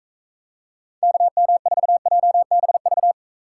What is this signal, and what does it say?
Morse code, heard as a keyed tone, about 41 wpm, 700 Hz. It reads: KM4JBV